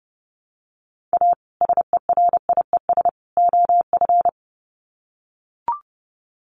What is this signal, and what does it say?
Morse code, heard as a keyed tone, about 30 wpm, 700 Hz. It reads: A SERIES OF
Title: Sets of 3 Words 30wpm